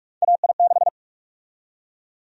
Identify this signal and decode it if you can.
Morse code, heard as a keyed tone, about 42 wpm, 700 Hz. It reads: AI6
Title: USA Call Sign Prefixes - 45wpm